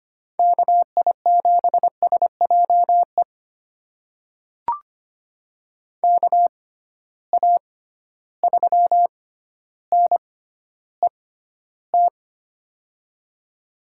Morse code, heard as a keyed tone, about 25 wpm, spaced as for 13 wpm, 700 Hz. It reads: KI7SJE KA3NET
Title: USA Call Signs - 6 Characters - 6x Character Spacing - 25wpm